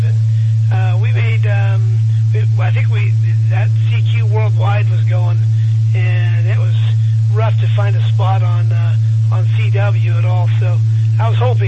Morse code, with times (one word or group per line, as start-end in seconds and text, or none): none